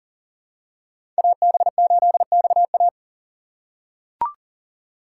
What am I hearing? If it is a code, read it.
Morse code, heard as a keyed tone, about 40 wpm, 700 Hz. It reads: AB8XA
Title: USA Call Signs - 5 Characters - 3x Character Spacing - 40wpm